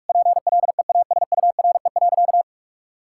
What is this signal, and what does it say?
Morse code, heard as a keyed tone, about 41 wpm, 700 Hz. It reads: PLEASURE.
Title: Sentences from Top 1600 Words 45wpm